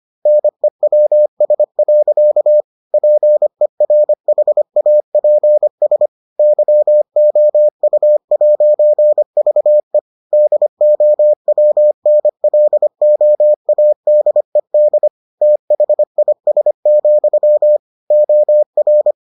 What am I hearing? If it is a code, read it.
Morse code, heard as a keyed tone, about 25 wpm, 600 Hz. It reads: NEWS. PERHAPS YOU'VE DOWNLOADED THIS, OR